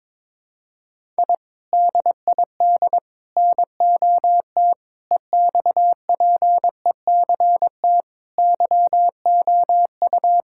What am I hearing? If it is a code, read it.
Morse code, heard as a keyed tone, about 22 wpm, 700 Hz. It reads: I DID NOT EXPECT YOU
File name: practice / 22wpm / Sentences with 3-Word Phrases - 22wpm